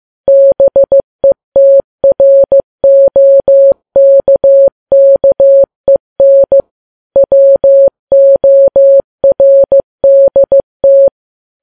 Morse code, heard as a keyed tone, about 15 wpm, 550 Hz. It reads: BETROKKEN WORDT
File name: Files Web tls / Morse on9fv / morse15kps